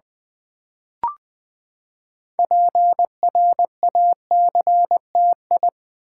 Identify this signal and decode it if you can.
Morse code, heard as a keyed tone, about 20 wpm, 700 Hz. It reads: PRACTI
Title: Build A Word - 6 Letters At A Time - 12-Letter Words - 20wpm